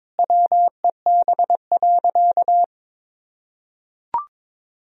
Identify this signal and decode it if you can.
Morse code, heard as a keyed tone, about 22 wpm, 700 Hz. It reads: WEB.
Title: Sentences from Top 1000 Words - Review Part 4 - 22wpm